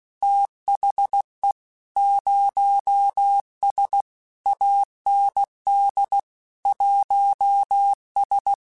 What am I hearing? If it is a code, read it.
Morse code, heard as a keyed tone, about 16 wpm, 800 Hz. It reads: THE 0S AND 1S
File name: Technical Subjects / Technical Subjects-mp3s / Gates Intro @15